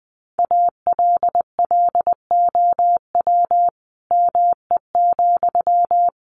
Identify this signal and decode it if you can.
Morse code, heard as a keyed tone, about 20 wpm, 700 Hz. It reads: ALLOW ME,
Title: Sentences from Top 700 Words 20wpm